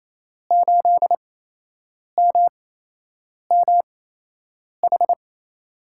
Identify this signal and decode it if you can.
Morse code, heard as a keyed tone, about 28 wpm, spaced as for 8 wpm, 700 Hz. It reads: 8MMH
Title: USA Call Signs - 6 Characters - 8x Character Spacing - 28wpm